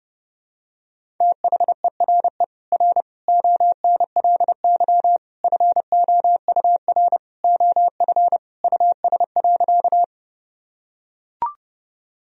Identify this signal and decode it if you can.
Morse code, heard as a keyed tone, about 32 wpm, 700 Hz. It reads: THERE R ONLY FOUR OF US.
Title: Sentences from Top 300 Words and Abbreviations 30wpm